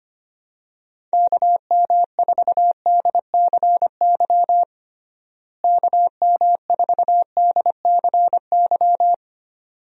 Morse code, heard as a keyed tone, about 25 wpm, 700 Hz. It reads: KM4DCY KM4DCY
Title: USA Call Signs - 6 Characters - Repeated 2x - 25wpm